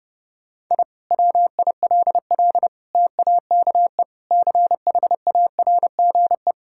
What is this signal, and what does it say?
Morse code, heard as a keyed tone, about 30 wpm, 700 Hz. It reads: I WILL TAKE CHARGE
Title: Sentences from Top 1000 Words - Review Part 6 - 30wpm